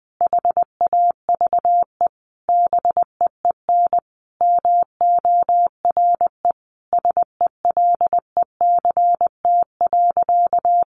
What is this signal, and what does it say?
Morse code, heard as a keyed tone, about 20 wpm, 700 Hz. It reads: HAVE BEEN MORE SELECT.
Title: Sentences from Top 1300 Words 20wpm